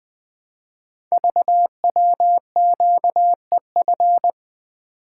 Morse code, heard as a keyed tone, about 20 wpm, 700 Hz. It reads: VWQEF
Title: Code Groups - 5 Letters - 20wpm